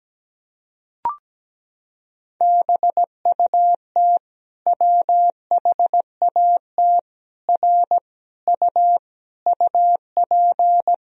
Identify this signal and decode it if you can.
Morse code, heard as a keyed tone, about 17 wpm, 700 Hz. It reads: BUT WHAT R U UP
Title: Sentences from Top 100 Words and Abbreviations 17wpm